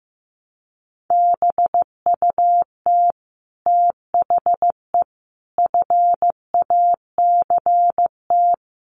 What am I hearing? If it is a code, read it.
Morse code, heard as a keyed tone, about 15 wpm, 700 Hz. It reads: BUT THE FACT
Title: Sets of 5 Words Encore2 15wpm